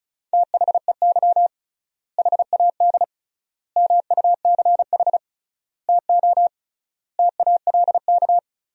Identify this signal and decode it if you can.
Morse code, heard as a keyed tone, about 36 wpm, 700 Hz. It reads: THEY HAD MUCH TO TALK
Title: 3x Word-Spacing - Sentences from Top 400 Words - 35wpm